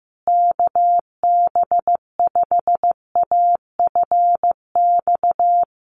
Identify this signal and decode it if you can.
Morse code, heard as a keyed tone, about 15 wpm, 700 Hz. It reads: KB5AFX
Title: USA Call Signs Repeated 3x 15wpm